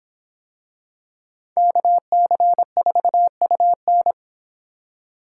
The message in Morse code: KC4UN

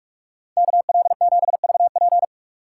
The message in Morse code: KL7VP

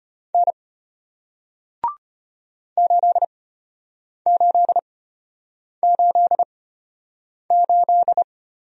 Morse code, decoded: N 8888

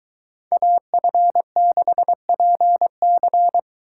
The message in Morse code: AF6PC

